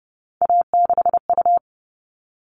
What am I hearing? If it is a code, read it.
Morse code, heard as a keyed tone, about 30 wpm, 700 Hz. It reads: A6U
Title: All Cut Numbers - 3-Digit 30wpm